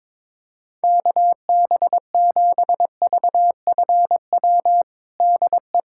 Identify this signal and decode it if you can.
Morse code, heard as a keyed tone, about 22 wpm, 700 Hz. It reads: KB7VFW DE
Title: USA Callsign DE Callsign - 6 Characters - 22wpm